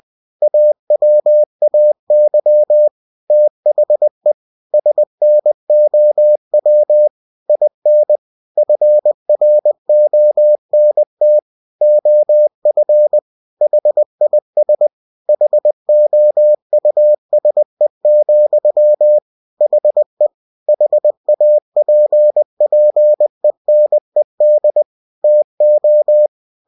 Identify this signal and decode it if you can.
Morse code, heard as a keyed tone, about 20 wpm, 600 Hz. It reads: AWAY THE SNOW IN FRONT OF HIS HOUSE, HE HAPPENED TO
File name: cw-books / winnie-the-pooh / winnie-20wpm-page-0021